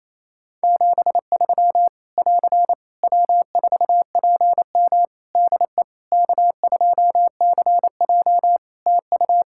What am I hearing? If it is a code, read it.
Morse code, heard as a keyed tone, about 27 wpm, 700 Hz. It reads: 73 <AR> W4PM DE K2CJ TU
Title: K1USN SST QSO Practice 28wpm